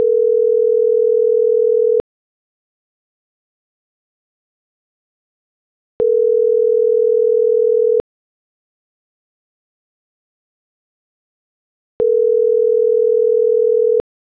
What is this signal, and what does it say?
Ringback tone (North America), 440 + 480 Hz.
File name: CENTRAL MULTIMODAL / out angel.velazquez